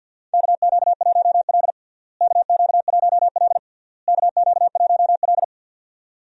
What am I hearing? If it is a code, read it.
Morse code, heard as a keyed tone, about 53 wpm, 700 Hz. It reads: KQ1L KQ1L KQ1L
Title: USA Call Signs - 4 Characters - Repeated 3x - 50wpm